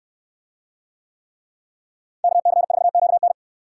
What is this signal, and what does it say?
Morse code, heard as a keyed tone, about 68 wpm, 700 Hz. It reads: KC4QN